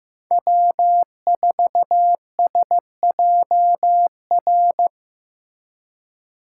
Morse code, heard as a keyed tone, about 18 wpm, 700 Hz. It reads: W4SJR
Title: Speed Racing - USA Call Signs 15wpm